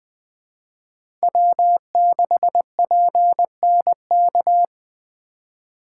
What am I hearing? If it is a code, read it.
Morse code, heard as a keyed tone, about 20 wpm, 700 Hz. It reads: W6PNK
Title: USA Call Signs - 5 Characters - 3x Character Spacing - 20wpm